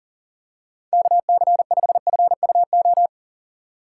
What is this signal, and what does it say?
Morse code, heard as a keyed tone, about 40 wpm, 700 Hz. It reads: KC5FUO
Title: USA Call Signs - 4x Character Spacing - 40wpm